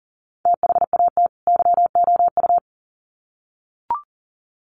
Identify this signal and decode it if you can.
Morse code, heard as a keyed tone, about 40 wpm, 700 Hz. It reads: THAT YOU
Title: Sets of 2 Words 40wpm